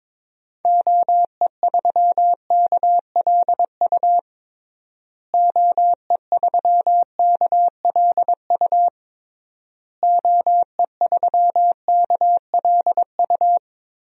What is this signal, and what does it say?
Morse code, heard as a keyed tone, about 22 wpm, 700 Hz. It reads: OE3KLU OE3KLU OE3KLU
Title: DXCC-100 Call Signs - 6 Characters - Repeated 3x - 22wpm